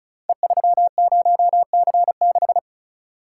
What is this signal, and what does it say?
Morse code, heard as a keyed tone, about 36 wpm, 700 Hz. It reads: E30C6
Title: Code Groups - 5 Letter:Numbers - 35wpm